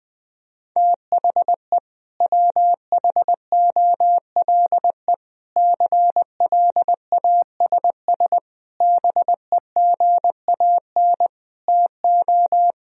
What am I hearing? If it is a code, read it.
Morse code, heard as a keyed tone, about 20 wpm, 700 Hz. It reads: THE WHOLE CLASS BEGAN TO
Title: Sentences from Top 1000 Words - Review Part 2 - 20wpm